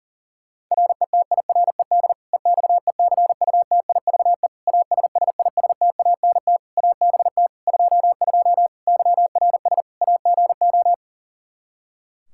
Morse code, heard as a keyed tone, about 40 wpm, 700 Hz. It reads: RETIRED EXECUTIVE ASSISTANT ABT 22 YRS AGO
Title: Practice QSO - Exchange 3 - More Info - 40wpm